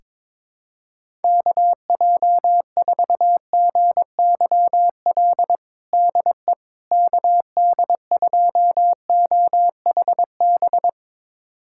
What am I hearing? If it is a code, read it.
Morse code, heard as a keyed tone, about 22 wpm, 700 Hz. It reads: KJ4GYL DE KD2OHB
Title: USA Callsign DE Callsign - 6 Characters - 22wpm